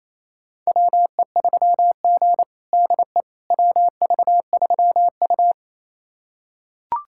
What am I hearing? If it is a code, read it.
Morse code, heard as a keyed tone, about 28 wpm, 700 Hz. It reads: WE3G DE WV3U
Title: USA Callsign DE Callsign - 4 Characters - 28wpm